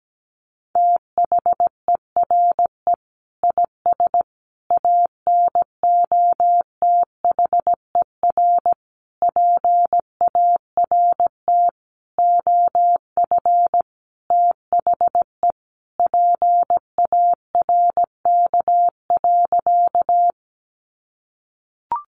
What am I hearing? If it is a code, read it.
Morse code, heard as a keyed tone, about 17 wpm, 700 Hz. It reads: THERE IS ANOTHER PART OF THE PARK.
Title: Sentences from Top 1300 Words 17wpm